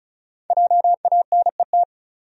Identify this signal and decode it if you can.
Morse code, heard as a keyed tone, about 36 wpm, 700 Hz. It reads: JANET
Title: USA Common Names - 35wpm